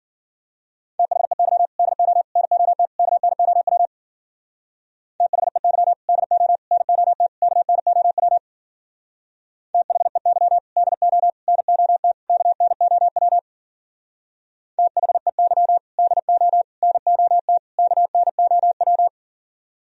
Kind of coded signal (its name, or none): Morse code